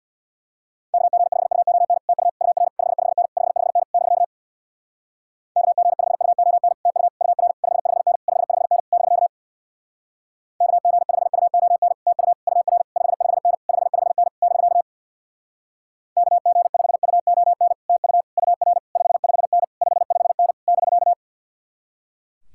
Morse code, heard as a keyed tone, about 64 wpm, 700 Hz. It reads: KG5UON TU UR 55N 55N <BK> KG5UON TU UR 55N 55N <BK> KG5UON TU UR 55N 55N <BK> KG5UON TU UR 55N 55N <BK>